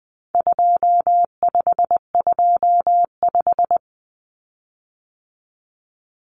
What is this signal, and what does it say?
Morse code, heard as a keyed tone, about 20 wpm, 700 Hz. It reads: 2525
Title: Common Cut Numbers - 4-Digit 20wpm